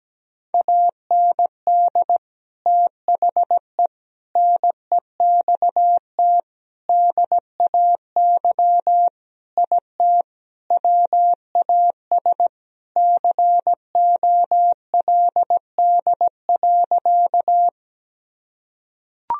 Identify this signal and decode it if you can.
Morse code, heard as a keyed tone, about 17 wpm, 700 Hz. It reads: AND THE NEXT DAY IT WAS COLD.
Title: Sentences from Top 1000 Words - Review Part 1 - 17wpm